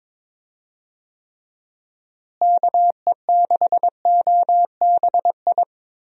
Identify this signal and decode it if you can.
Morse code, heard as a keyed tone, about 22 wpm, 700 Hz. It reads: KE6OBI
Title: USA Call Signs - 6 Characters - Repeated 3x - 22wpm